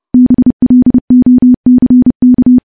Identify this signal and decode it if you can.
Morse code, heard as a keyed tone, about 32 wpm, 250 Hz. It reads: BLOCK
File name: morse / morse-training-30WPM